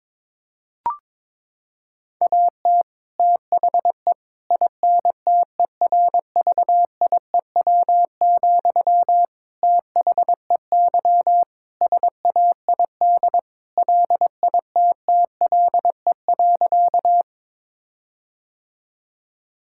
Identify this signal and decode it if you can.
Morse code, heard as a keyed tone, about 22 wpm, 700 Hz. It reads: AT THE INTERVIEW, THEY SAID LITTLE.
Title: Sentences from Top 1600 Words 22wpm